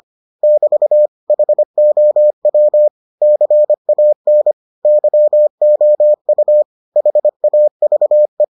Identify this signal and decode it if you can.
Morse code, heard as a keyed tone, about 25 wpm, 600 Hz. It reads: <BT> HOW CAN YOU HAVE